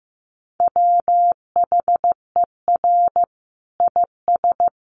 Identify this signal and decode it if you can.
Morse code, heard as a keyed tone, about 14 wpm, 700 Hz. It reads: WHER IS
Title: Sentences from Top 1500 Words 15wpm